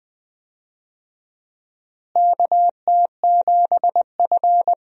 Morse code, heard as a keyed tone, about 20 wpm, 700 Hz. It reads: KT7F
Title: USA Call Signs - 4 Characters - Repeated 2x - 20wpm